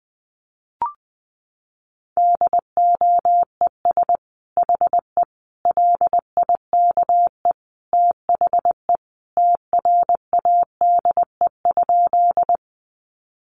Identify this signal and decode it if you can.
Morse code, heard as a keyed tone, about 19 wpm, 700 Hz. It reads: DOES HE LIKE THE TRADE?
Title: Sentences from Top 600 Words 20wpm